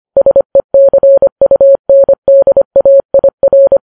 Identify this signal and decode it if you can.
Morse code, heard as a keyed tone, about 25 wpm, 550 Hz. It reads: SECUNDAIR
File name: Files Web tls / Morse on9fv / morse25kps